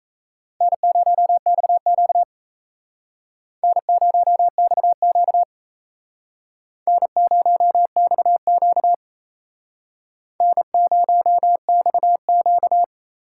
Morse code, heard as a keyed tone, about 35 wpm, 700 Hz. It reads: N0XQ N0XQ N0XQ N0XQ
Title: Speed Racing - USA Call Signs 28wpm